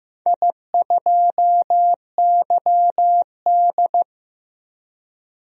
Morse code, heard as a keyed tone, about 15 wpm, 700 Hz. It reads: I2YD
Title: DXCC-100 Call Signs - 4 Characters - 15wpm